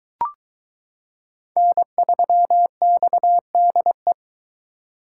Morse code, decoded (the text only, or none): N3XDE